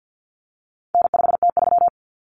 Morse code, heard as a keyed tone, about 50 wpm, 700 Hz. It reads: N5T3
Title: Common Cut Numbers - 4-Digit 50wpm